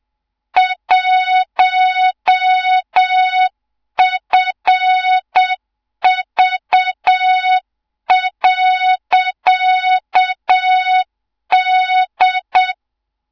Morse code, heard as a keyed tone, about 7 wpm, 3700 Hz, with noise. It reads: 1FV.D
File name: Morze-HA2MN / hang / 24 GYAKORLAT